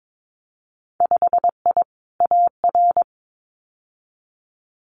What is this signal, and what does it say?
Morse code, heard as a keyed tone, about 22 wpm, 700 Hz. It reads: 5I AR
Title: Winter Field Day Exchanges Encore 22wpm